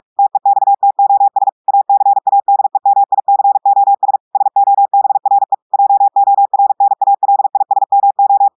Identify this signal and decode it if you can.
Morse code, heard as a keyed tone, about 46 wpm, 800 Hz. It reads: TEXTOS ACADEMICOS SOBRE JORNALISMO
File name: portugais texte en clair 1 vitesse 45 mots minutes